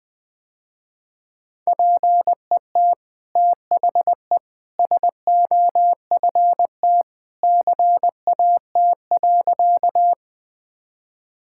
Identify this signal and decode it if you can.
Morse code, heard as a keyed tone, about 20 wpm, 700 Hz. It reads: PET THE SOFT CAT.
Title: Easy Sentences with Hints - 20wpm